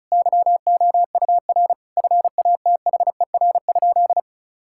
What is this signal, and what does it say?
Morse code, heard as a keyed tone, about 34 wpm, 700 Hz. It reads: YOUR FATHER?
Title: Sentences from Top 1600 Words 35wpm